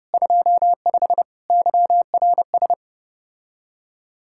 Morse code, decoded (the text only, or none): 25 YRS